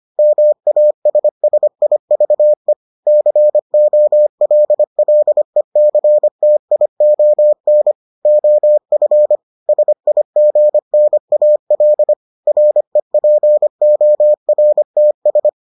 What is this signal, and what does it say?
Morse code, heard as a keyed tone, about 25 wpm, 600 Hz. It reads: MASSIVE COLLECTION OF SIGNAL REPORTS